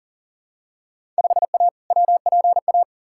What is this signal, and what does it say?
Morse code, heard as a keyed tone, about 40 wpm, 700 Hz. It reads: 5A WPA
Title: Field Day Exchanges Encore - 40wpm